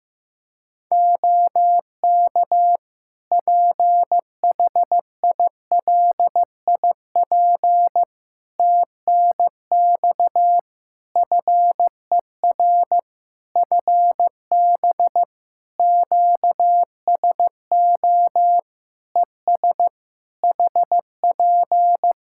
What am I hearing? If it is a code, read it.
Morse code, heard as a keyed tone, about 15 wpm, 700 Hz. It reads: OK PHILIP TNX FER FB QSO ES HP
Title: K1USN SST QSO Practice 15wpm